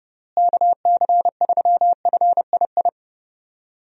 Morse code, heard as a keyed tone, about 30 wpm, 700 Hz. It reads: KC3FII
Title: USA Call Signs - 6 Characters - Repeated 2x - 30wpm